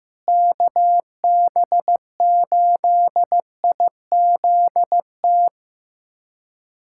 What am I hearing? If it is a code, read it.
Morse code, heard as a keyed tone, about 15 wpm, 700 Hz. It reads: KB8IZT
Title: USA Callsign DE Callsign 15wpm